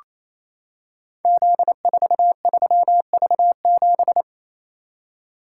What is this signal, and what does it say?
Morse code, heard as a keyed tone, about 29 wpm, 700 Hz. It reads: Z43V7
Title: Code Groups - 5 Letter:Numbers - 28wpm